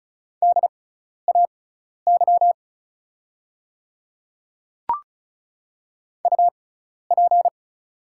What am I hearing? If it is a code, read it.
Morse code, heard as a keyed tone, about 36 wpm, spaced as for 12 wpm, 690 Hz. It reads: DAY UP
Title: 6x Character Spacing - Top 100 Words - 35wpm